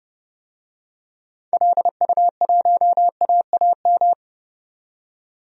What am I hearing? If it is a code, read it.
Morse code, heard as a keyed tone, about 30 wpm, 700 Hz. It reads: LU1AAM
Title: DXCC-100 Call Signs - 6 Characters - 30wpm